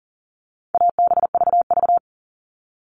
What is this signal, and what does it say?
Morse code, heard as a keyed tone, about 40 wpm, 700 Hz. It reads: ABVV
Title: All Cut Numbers - 4-Digit 40wpm